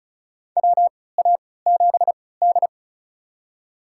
Morse code, heard as a keyed tone, about 36 wpm, spaced as for 20 wpm, 700 Hz. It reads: WA7D
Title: USA Call Signs - 4 Characters - 3x Character Spacing - 35wpm